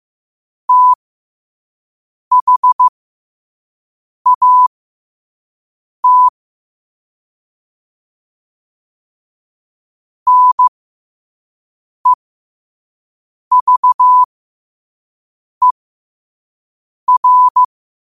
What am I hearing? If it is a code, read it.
Morse code, heard as a keyed tone, about 15 wpm, spaced as for 5 wpm, 1000 Hz. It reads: THAT NEVER
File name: qotdfiles / 5wpm